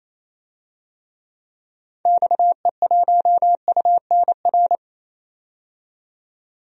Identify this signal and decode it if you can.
Morse code, heard as a keyed tone, about 27 wpm, 700 Hz. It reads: XE1UNR
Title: DXCC-100 Call Signs - 6 Characters - 28wpm